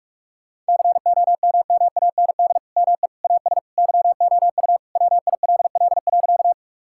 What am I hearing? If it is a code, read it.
Morse code, heard as a keyed tone, about 46 wpm, 700 Hz. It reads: COMMAND ME AS YOU WILL.
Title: Sentences from Top 1600 Words 45wpm